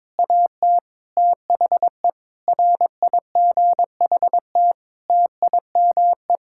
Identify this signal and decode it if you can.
Morse code, heard as a keyed tone, about 22 wpm, 700 Hz. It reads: AT THE RIGHT TIME
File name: Sets of 9 Words - 22wpm